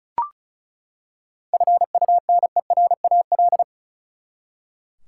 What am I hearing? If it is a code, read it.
Morse code, heard as a keyed tone, about 36 wpm, 700 Hz. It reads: FUNERAL
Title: Seven Letter Words - 35wpm